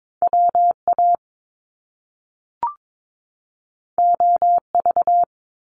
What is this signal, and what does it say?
Morse code, heard as a keyed tone, about 22 wpm, 700 Hz. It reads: WA OV
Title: Top Two Letter Combinations in Words 22wpm